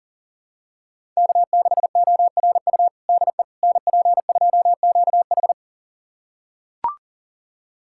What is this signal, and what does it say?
Morse code, heard as a keyed tone, about 40 wpm, 700 Hz. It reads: K6ORU DE NP2QH
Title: USA Callsign DE Callsign - 5 Characters - 40wpm